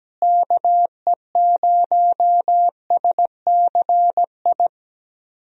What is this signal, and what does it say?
Morse code, heard as a keyed tone, about 17 wpm, 700 Hz. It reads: KE0SCI
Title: USA Call Signs - 6 Characters - Repeated 2x - 17wpm